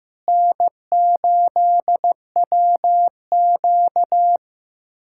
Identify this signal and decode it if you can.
Morse code, heard as a keyed tone, about 16 wpm, 700 Hz. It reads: N8WQ
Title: USA Callsign DE Callsign 15wpm